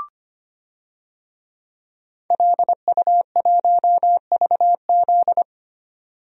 Morse code, heard as a keyed tone, about 25 wpm, 700 Hz. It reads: LU1VZ
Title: DXCC-100 Call Signs - 5 Characters - 25wpm